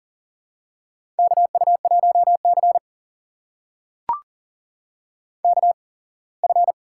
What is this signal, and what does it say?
Morse code, heard as a keyed tone, about 43 wpm, spaced as for 25 wpm, 700 Hz. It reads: KU1C KF
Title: USA Call Signs - 4 Characters - 8x Character Spacing - 40wpm